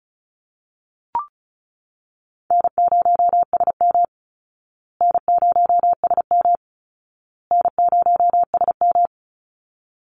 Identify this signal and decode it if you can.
Morse code, heard as a keyed tone, about 34 wpm, 700 Hz. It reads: N0SM N0SM N0SM
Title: Call Signs Repeated 2x - 35wpm